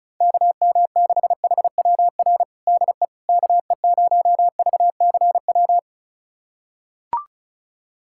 Morse code, heard as a keyed tone, about 34 wpm, 700 Hz. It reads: KM6HWR DE KE0VCW
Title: USA Callsign DE Callsign - 6 Characters - 35wpm